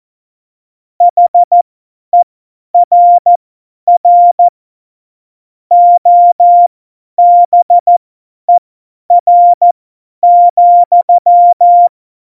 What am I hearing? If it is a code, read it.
Morse code, heard as a keyed tone, about 14 wpm, spaced as for 10 wpm, 700 Hz. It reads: HERR OBER,
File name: Morsekurs / Witze / Witz 17. September 2019, 50 F70